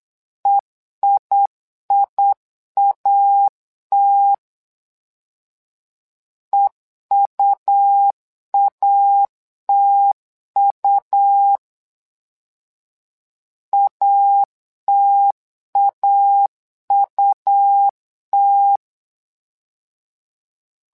Morse code, heard as a keyed tone, about 8 wpm, 800 Hz. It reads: EIIAT EUATU ATAUT
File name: LEZIONI / 2 UHDN / 2 U 8wpm 2028